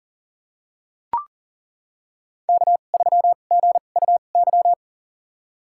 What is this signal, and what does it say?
Morse code, heard as a keyed tone, about 40 wpm, spaced as for 29 wpm, 700 Hz. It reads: K3GUY
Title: USA Call Signs - 2x Character Spacing - 40wpm